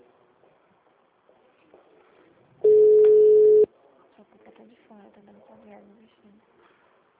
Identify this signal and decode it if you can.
Ringback tone (Europe, 425 Hz), with speech.
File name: out- -2016- - - .9406